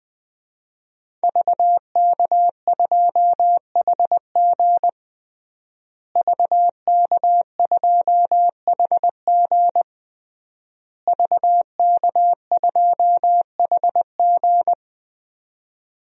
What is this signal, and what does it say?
Morse code, heard as a keyed tone, about 20 wpm, 700 Hz. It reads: VK2HG VK2HG VK2HG
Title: DXCC-100 Call Signs - 5 Characters - Repeated 3x - 20wpm